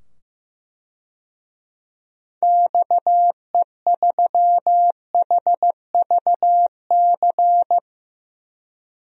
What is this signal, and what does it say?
Morse code, heard as a keyed tone, about 15 wpm, 700 Hz. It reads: XE3HVC
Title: DXCC-100 Call Signs - 6 Characters - 15wpm